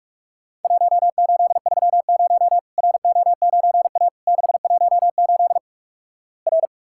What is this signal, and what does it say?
Morse code, heard as a keyed tone, about 47 wpm, 700 Hz. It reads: 1830 RO9A 618 R
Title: Worked All Europe DX Contest - QTC Exchange - 45wpm